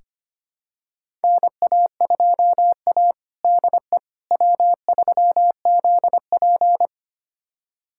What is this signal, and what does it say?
Morse code, heard as a keyed tone, about 25 wpm, 700 Hz. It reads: NA2A DE W3ZP
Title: USA Callsign DE Callsign - 4 Characters - 25wpm